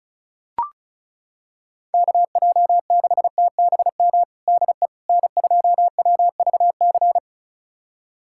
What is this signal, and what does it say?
Morse code, heard as a keyed tone, about 34 wpm, 700 Hz. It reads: KJ6TBM DE N2WVC
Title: USA Callsign DE Callsign 35wpm